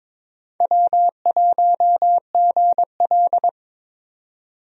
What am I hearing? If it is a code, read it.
Morse code, heard as a keyed tone, about 22 wpm, 700 Hz. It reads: W1GL
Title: USA Call Signs - 4 Characters - 22wpm